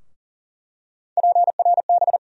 Morse code, heard as a keyed tone, about 40 wpm, 700 Hz. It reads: PRB
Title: USA Call Sign Postfixes - 3 Letters 40wpm